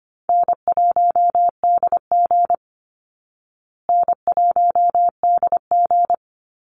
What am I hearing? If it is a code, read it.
Morse code, heard as a keyed tone, about 25 wpm, 700 Hz. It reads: N1DG N1DG
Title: Easy Call Signs Repeated 2x - 25wpm